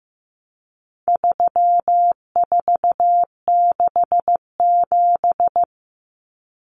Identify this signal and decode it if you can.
Morse code, heard as a keyed tone, about 15 wpm, 700 Hz. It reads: 3467
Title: Numbers - 4-Digit 15wpm